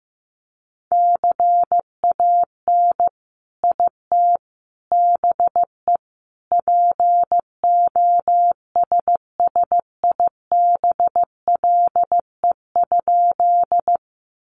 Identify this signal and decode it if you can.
Morse code, heard as a keyed tone, about 15 wpm, 700 Hz. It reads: CAN IT BE POSSIBLE?
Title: Sentences from Top 500 Words 15wpm